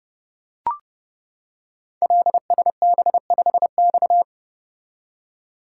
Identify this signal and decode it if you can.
Morse code, heard as a keyed tone, about 29 wpm, 700 Hz. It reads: LSB5X
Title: Code Groups - 5 Letter:Numbers - 30wpm